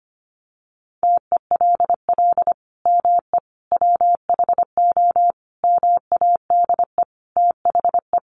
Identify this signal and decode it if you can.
Morse code, heard as a keyed tone, about 25 wpm, 700 Hz. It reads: TELL ME WHO MADE THE
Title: Sentences from Top 200 Words 25wpm